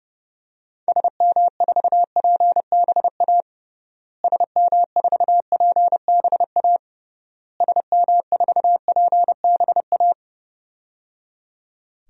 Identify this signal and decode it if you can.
Morse code, heard as a keyed tone, about 30 wpm, 700 Hz. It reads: SM4PBA SM4PBA SM4PBA
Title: DXCC-100 Call Signs - 6 Characters - Repeated 3x - 30wpm